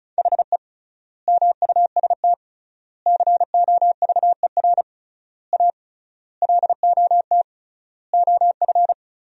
Morse code, heard as a keyed tone, about 34 wpm, 700 Hz. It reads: HE MUST COVER A LOT OF
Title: 3x Word-Spacing - Sentences from Top 500 Words - 35wpm